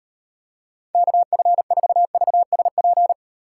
Morse code, heard as a keyed tone, about 38 wpm, 700 Hz. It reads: KF4VSP